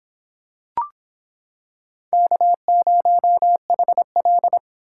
Morse code, heard as a keyed tone, about 26 wpm, 700 Hz. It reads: K0HL